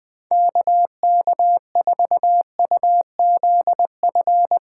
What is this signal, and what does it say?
Morse code, heard as a keyed tone, about 20 wpm, 700 Hz. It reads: KK4UZF
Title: Speed Racing - USA Call Signs 17wpm